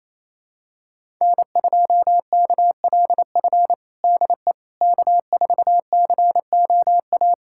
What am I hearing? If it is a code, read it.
Morse code, heard as a keyed tone, about 28 wpm, 700 Hz. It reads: N2KLF DE K4COA
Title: USA Callsign DE Callsign - 5 Characters - 28wpm